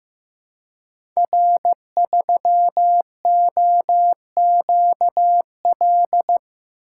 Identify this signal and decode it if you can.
Morse code, heard as a keyed tone, about 15 wpm, 700 Hz. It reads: R3OQL
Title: Code Groups - 5 Letter:Numbers - 15wpm